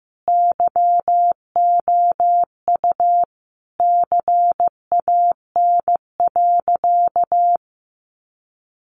Morse code, heard as a keyed tone, about 15 wpm, 700 Hz. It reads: YOU CAN.
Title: Sentences from Top 1100 Words 15wpm